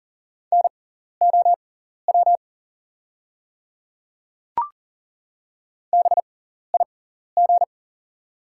Morse code, heard as a keyed tone, about 40 wpm, spaced as for 14 wpm, 700 Hz. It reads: NOW BIG
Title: 6x Character Spacing - Top 200 Words - 40wpm